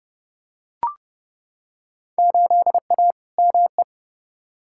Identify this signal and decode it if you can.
Morse code, heard as a keyed tone, about 30 wpm, 700 Hz. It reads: 8A ME
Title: Field Day Exchanges - 30wpm